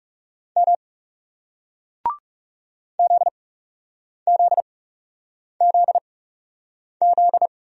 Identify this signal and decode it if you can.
Morse code, heard as a keyed tone, about 40 wpm, spaced as for 8 wpm, 700 Hz. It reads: M ZZZZ